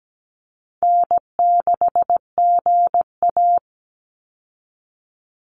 Morse code, heard as a keyed tone, about 17 wpm, 700 Hz. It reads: N6GA
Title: Easy Call Signs Repeated 3x - 17wpm